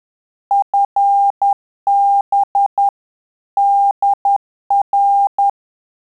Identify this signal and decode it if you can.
Morse code, heard as a keyed tone, about 10 wpm, 800 Hz. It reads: FB DR
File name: Telegraphy-group / QSO CW 10 PPM